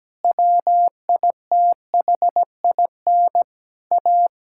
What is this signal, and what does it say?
Morse code, heard as a keyed tone, about 17 wpm, 700 Hz. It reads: WITHIN A
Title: Sentences from Top 1000 Words - Review Part 5 - 17wpm